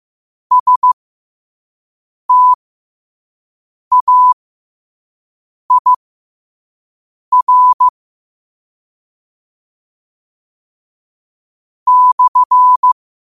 Morse code, heard as a keyed tone, about 15 wpm, spaced as for 5 wpm, 1000 Hz. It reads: STAIR /
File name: qotdfiles / 5wpm